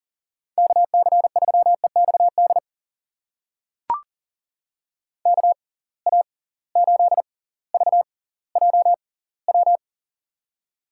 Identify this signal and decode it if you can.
Morse code, heard as a keyed tone, about 40 wpm, spaced as for 20 wpm, 700 Hz. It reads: KC3EXD KA8VJW
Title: USA Call Signs - 6 Characters - 6x Character Spacing - 40wpm